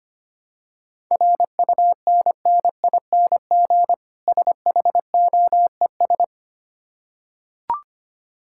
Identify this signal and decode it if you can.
Morse code, heard as a keyed tone, about 25 wpm, 700 Hz. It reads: RUNNING SHOES
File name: practice / 25wpm / Difficult Words with Hints - 25wpm